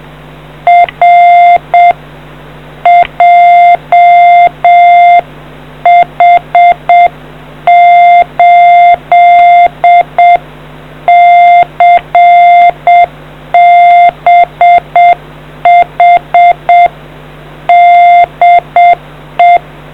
Morse code, heard as a keyed tone, about 7 wpm, 2100 Hz. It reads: R JH8CBH DE